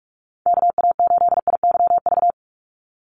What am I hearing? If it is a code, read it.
Morse code, heard as a keyed tone, about 44 wpm, 700 Hz. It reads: KA8IYV
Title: USA Call Signs 45wpm